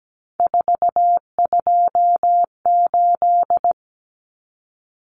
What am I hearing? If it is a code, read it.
Morse code, heard as a keyed tone, about 17 wpm, 700 Hz. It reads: 428
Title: Common Cut Numbers - 3-Digit 17wpm